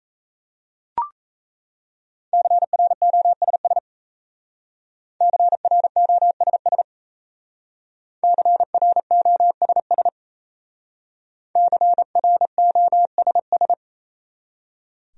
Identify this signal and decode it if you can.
Morse code, heard as a keyed tone, about 35 wpm, 700 Hz. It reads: CROSS CROSS CROSS CROSS